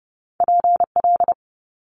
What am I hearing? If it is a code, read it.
Morse code, heard as a keyed tone, about 30 wpm, 700 Hz. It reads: PL
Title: Multi-letter Phonemes Quick Fire 30wpm